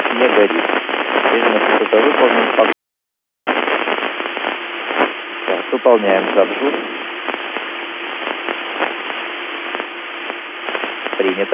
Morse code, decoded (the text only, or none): none